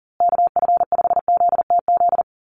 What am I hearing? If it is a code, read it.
Morse code, heard as a keyed tone, about 40 wpm, 700 Hz. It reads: KF5ZTZ
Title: USA Call Signs Repeated 3x 40wpm